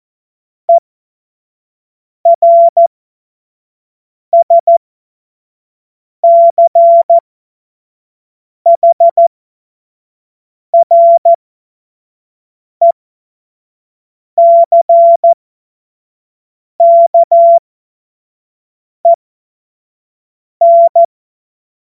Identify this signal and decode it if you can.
Morse code, heard as a keyed tone, about 14 wpm, spaced as for 5 wpm, 680 Hz. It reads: ERSCHRECKEN
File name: Morsekurs / Witze / Witz 14. Februar 2023 25 F70